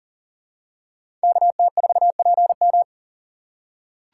Morse code, heard as a keyed tone, about 38 wpm, 700 Hz. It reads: KT4PM
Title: USA Call Signs - 3x Character Spacing - 40wpm